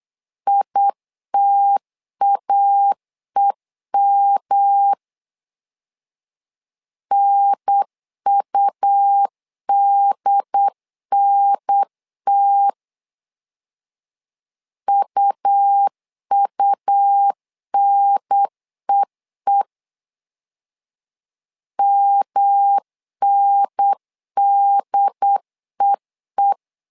Morse code, heard as a keyed tone, about 8 wpm, 800 Hz. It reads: ITAEM NUDNT UUNEE MNDEE
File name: LEZIONI / 3 MWRO / 3 M 8wpm 2300